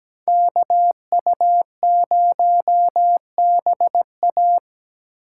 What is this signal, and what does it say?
Morse code, heard as a keyed tone, about 20 wpm, 700 Hz. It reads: KU0BA
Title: Speed Racing - USA 2x2 Call Signs 17wpm